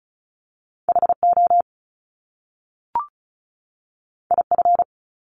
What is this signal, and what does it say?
Morse code, heard as a keyed tone, about 36 wpm, 700 Hz. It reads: HO IF
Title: Top Two Letter Combinations in Words 35wpm